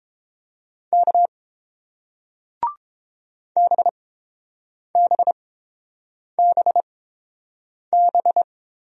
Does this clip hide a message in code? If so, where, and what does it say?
Yes: Morse code from 0.9 to 8.4 s, heard as a keyed tone, about 30 wpm, spaced as for 7 wpm, 700 Hz. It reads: K BBBB